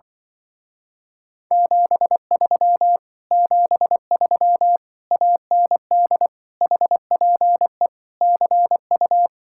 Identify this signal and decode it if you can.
Morse code, heard as a keyed tone, about 25 wpm, 700 Hz. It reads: 73 73 AND HPE CU